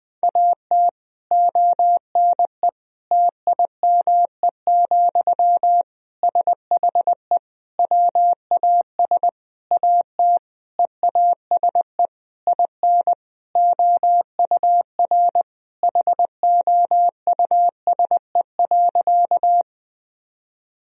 Morse code, heard as a keyed tone, about 20 wpm, 700 Hz. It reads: AT ONE TIME, SHE WAS AT EASE IN OUR HOUSE.
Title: Sentences from Top 1000 Words - Review Part 1 - 20wpm